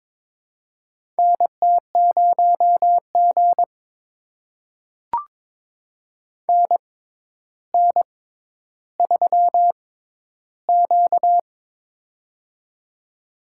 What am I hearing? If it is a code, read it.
Morse code, heard as a keyed tone, about 22 wpm, spaced as for 11 wpm, 700 Hz. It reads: NT0G NN3Q
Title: USA Call Signs - 4 Characters - 6x Character Spacing - 22wpm